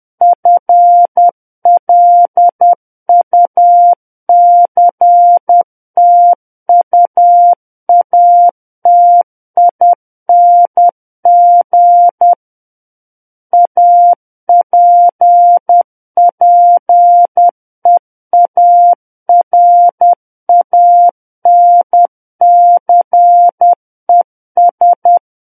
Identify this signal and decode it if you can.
Morse code, heard as a keyed tone, about 10 wpm, 700 Hz. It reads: FLUCTUATING APPEARANCES